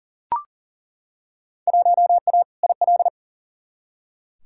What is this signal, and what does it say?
Morse code, heard as a keyed tone, about 40 wpm, 700 Hz. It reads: 1A IL
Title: Field Day Exchanges - 40wpm